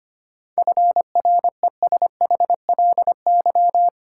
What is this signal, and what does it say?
Morse code, heard as a keyed tone, about 25 wpm, 700 Hz. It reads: FRESHLY
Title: Sets of 5 Words Encore - New Word Drill - 25wpm